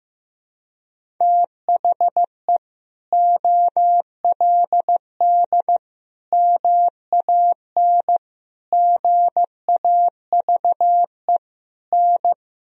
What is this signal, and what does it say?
Morse code, heard as a keyed tone, about 15 wpm, 700 Hz. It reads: THE OLD MAN GAVE N
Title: Sentences from Top 500 Words and Abbreviations 15wpm